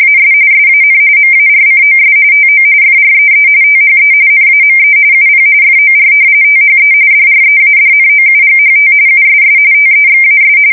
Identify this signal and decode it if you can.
RTTY, about 45.45 baud. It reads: AS SAN MARCOS TO FREE THE NAME UP FOR THE DREADNOUGHT USS TEXAS
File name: fsk170 / battleships